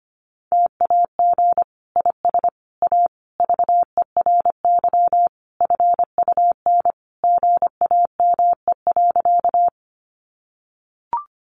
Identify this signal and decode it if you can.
Morse code, heard as a keyed tone, about 25 wpm, 700 Hz. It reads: TAG IS A VERY FUN GAME.
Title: Additional Easy Phrases 25wpm